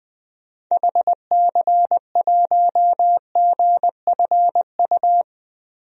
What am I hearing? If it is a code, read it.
Morse code, heard as a keyed tone, about 20 wpm, 700 Hz. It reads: HC1GFU
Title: DXCC-100 Call Signs - 6 Characters - Repeated 3x - 20wpm